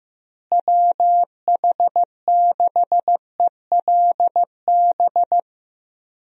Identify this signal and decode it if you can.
Morse code, heard as a keyed tone, about 15 wpm, 700 Hz. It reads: WH6ELB
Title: USA Call Signs - 6 Characters - Repeated 3x - 15wpm